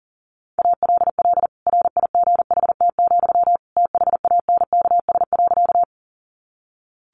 Morse code, heard as a keyed tone, about 40 wpm, 700 Hz. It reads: ALL RIGHT, THANKS.
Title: Sentences from Top 600 Words 40wpm